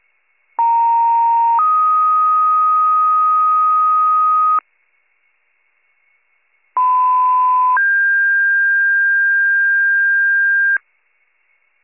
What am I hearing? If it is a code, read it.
Two-tone page, heard as a one-second tone, then a three-second tone, about 930 then 1250 Hz, about 1010 then 1600 Hz. It is heard from 0.6 to 10.8 s.